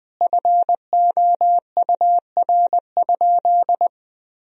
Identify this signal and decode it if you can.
Morse code, heard as a keyed tone, about 21 wpm, 700 Hz. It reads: FOUR?
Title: Sentences from Top 1000 Words - Review Part 1 - 20wpm